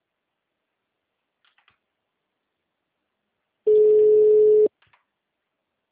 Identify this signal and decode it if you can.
Ringback tone (Europe, 425 Hz).